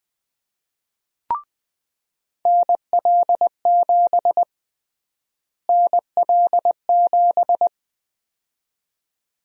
Morse code, heard as a keyed tone, about 21 wpm, 700 Hz. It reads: NL7 NL7
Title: USA Call Sign Prefixes - Repeated 2x - 20wpm